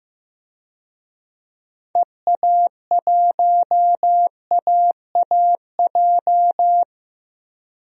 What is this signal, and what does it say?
Morse code, heard as a keyed tone, about 15 wpm, 700 Hz. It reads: EA1AAJ
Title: DXCC-100 Call Signs - 6 Characters - Repeated 2x - 15wpm